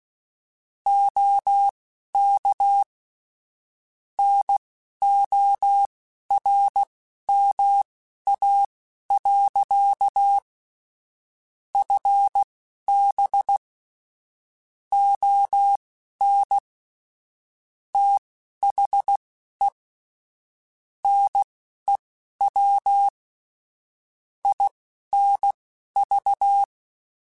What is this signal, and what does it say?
Morse code, heard as a keyed tone, about 16 wpm, spaced as for 11 wpm, 800 Hz. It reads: OK NORMA. FB ON THE NEW INV
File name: QSOs / QSOs-mp3s / WD0FBT 1@10